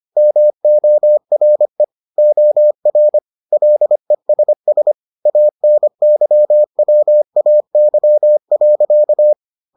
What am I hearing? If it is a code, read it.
Morse code, heard as a keyed tone, about 25 wpm, 600 Hz. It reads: MORE OR LESS ANYWAY.